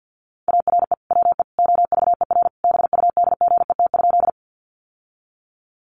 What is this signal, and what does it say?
Morse code, heard as a keyed tone, about 50 wpm, 700 Hz. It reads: ARE WE OVER BUDGET?
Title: Sentences from Top 1200 Words 50wpm